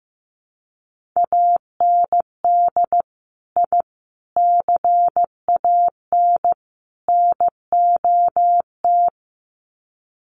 Morse code, heard as a keyed tone, about 15 wpm, 700 Hz. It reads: AND I CAN NOT
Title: Sets of 4 Words 15wpm